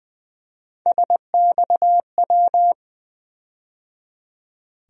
Morse code, heard as a keyed tone, about 21 wpm, 700 Hz. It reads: SXW
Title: USA Call Sign Postfixes - 3 Letters 20wpm